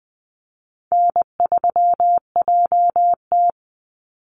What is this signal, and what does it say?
Morse code, heard as a keyed tone, about 20 wpm, 700 Hz. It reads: N3JT
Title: Call Signs Repeated 2x - 20wpm